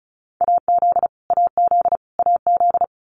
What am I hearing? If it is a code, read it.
Morse code, heard as a keyed tone, about 34 wpm, 700 Hz. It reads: AZ AZ AZ
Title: US States Repeated 3x 35wpm